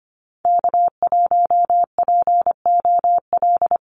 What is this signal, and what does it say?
Morse code, heard as a keyed tone, about 25 wpm, 700 Hz. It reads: K1POL
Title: USA Call Signs 25wpm